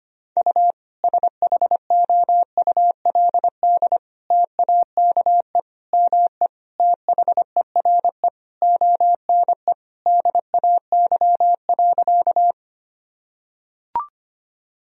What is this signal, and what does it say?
Morse code, heard as a keyed tone, about 25 wpm, 700 Hz. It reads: U SHOULD TAKE ME THERE ONE DAY.
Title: Sentences from Top 300 Words and Abbreviations 25wpm